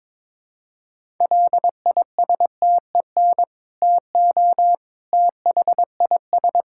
Morse code, heard as a keyed tone, about 22 wpm, 700 Hz. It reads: LISTEN TO THIS
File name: practice / 22wpm / Sentences with 2-Word Phrases - 22wpm